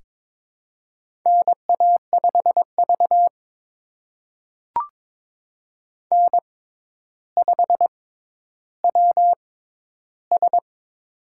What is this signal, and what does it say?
Morse code, heard as a keyed tone, about 23 wpm, spaced as for 11 wpm, 700 Hz. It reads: NA5V N5WS
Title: USA Call Signs - 4 Characters - 6x Character Spacing - 22wpm